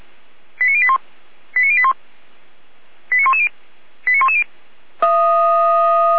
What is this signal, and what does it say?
Five-tone selective call (ZVEI), heard as quick runs of five tones, tones of about 70 ms. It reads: 89081 89081 89110 89110
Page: SDR/ZVEI